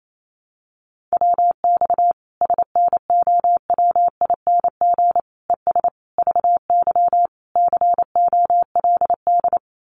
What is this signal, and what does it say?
Morse code, heard as a keyed tone, about 28 wpm, 700 Hz. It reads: WX SNOWING ES VY COLD
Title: Common QSO Phrases 28wpm